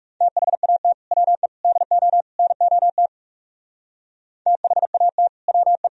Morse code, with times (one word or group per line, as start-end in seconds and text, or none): 0.2-0.9 THAT
1.1-1.5 WE
1.6-2.2 DO
2.4-3.1 NOT
4.5-5.3 THAT
5.5-5.9 WE